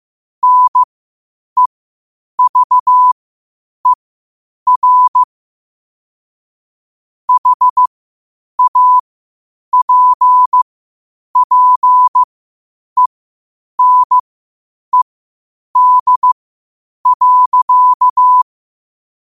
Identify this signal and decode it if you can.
Morse code, heard as a keyed tone, about 15 wpm, spaced as for 8 wpm, 1000 Hz. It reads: NEVER HAPPENED.